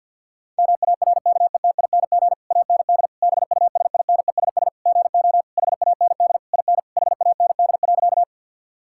Morse code, heard as a keyed tone, about 50 wpm, 700 Hz. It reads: MARKETING AND BUSINESS GO HAND IN HAND.
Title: Sentences from Top 1000 Words - Review Part 5 - 50wpm